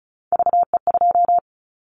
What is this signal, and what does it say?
Morse code, heard as a keyed tone, about 34 wpm, 700 Hz. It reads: VE2
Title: Callsign Prefixes for Top 100 DXCC - 35wpm